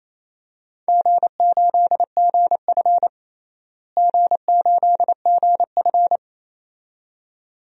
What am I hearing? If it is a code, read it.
Morse code, heard as a keyed tone, about 28 wpm, 700 Hz. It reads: G8GF G8GF
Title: DXCC-100 Call Signs - 4 Characters - Repeated 2x - 28wpm